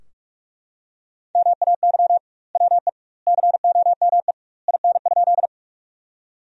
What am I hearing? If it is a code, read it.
Morse code, heard as a keyed tone, about 41 wpm, 700 Hz. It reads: MAY WE COME IN?
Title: 2x Word-Spacing - Sentences from Top 100 Words - 45wpm